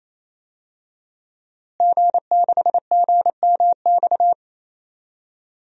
Morse code, heard as a keyed tone, about 28 wpm, 700 Hz. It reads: G6GMX
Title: DXCC-100 Call Signs - 5 Characters - Repeated 3x - 28wpm